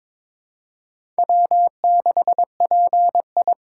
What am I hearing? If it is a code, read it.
Morse code, heard as a keyed tone, about 22 wpm, 700 Hz. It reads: W6PI
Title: USA Call Signs - 4 Characters - 3x Character Spacing - 22wpm